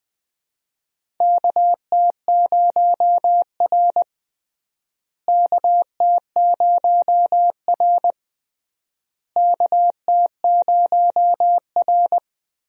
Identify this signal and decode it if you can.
Morse code, heard as a keyed tone, about 20 wpm, 700 Hz. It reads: KT0R KT0R KT0R
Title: USA Call Signs - 4 Characters - Repeated 3x - 20wpm